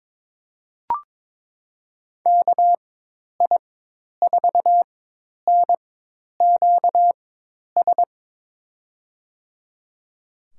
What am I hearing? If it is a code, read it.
Morse code, heard as a keyed tone, about 21 wpm, spaced as for 10 wpm, 700 Hz. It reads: KI4NQS
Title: USA Call Signs - 6 Characters - 4x Character Spacing - 22wpm